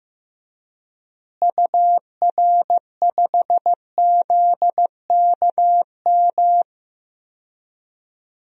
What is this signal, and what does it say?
Morse code, heard as a keyed tone, about 15 wpm, 700 Hz. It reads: UR5ZKM
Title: DXCC-100 Call Signs - 6 Characters - Repeated 3x - 15wpm